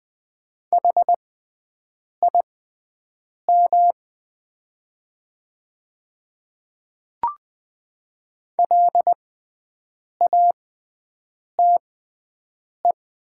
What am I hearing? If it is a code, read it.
Morse code, heard as a keyed tone, about 20 wpm, spaced as for 7 wpm, 700 Hz. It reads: HIM LATE
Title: 6x Character Spacing - Top 300 Words - 20wpm